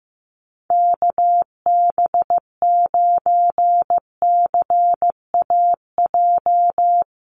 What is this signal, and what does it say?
Morse code, heard as a keyed tone, about 14 wpm, 700 Hz. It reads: KB9CAJ
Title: USA Call Signs Repeated 2x 15wpm